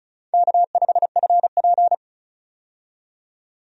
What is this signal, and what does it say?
Morse code, heard as a keyed tone, about 34 wpm, 700 Hz. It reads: K5FP
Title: USA Call Signs - 4 Characters - 35wpm